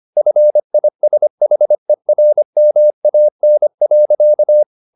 Morse code, heard as a keyed tone, about 25 wpm, 600 Hz. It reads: FISHERMAN.
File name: the old man and the sea part 1 25wpm